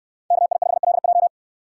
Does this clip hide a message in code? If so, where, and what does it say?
Yes: Morse code from 0.3 to 1.3 s, heard as a keyed tone, about 66 wpm, 690 Hz. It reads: KI5LJ